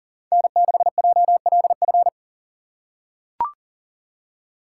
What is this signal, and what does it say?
Morse code, heard as a keyed tone, about 40 wpm, 700 Hz. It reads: N6JLF
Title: USA Call Signs - 5 Characters - 4x Character Spacing - 40wpm